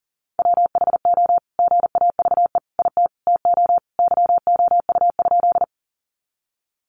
Morse code, heard as a keyed tone, about 40 wpm, 700 Hz. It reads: WHO GAVE IT TO YOU?
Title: Sentences from Top 500 Words 40wpm